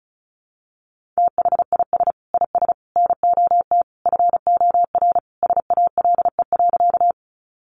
Morse code, heard as a keyed tone, about 34 wpm, 700 Hz. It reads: THIS IS NOT FOR SALE.
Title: Sentences from Top 1400 Words 35wpm